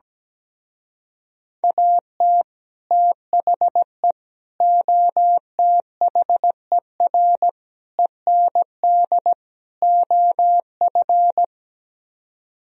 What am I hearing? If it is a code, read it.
Morse code, heard as a keyed tone, about 17 wpm, 700 Hz. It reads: AT THE OTHER END OF
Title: Speed-Racing - Sets of 5 Words 17wpm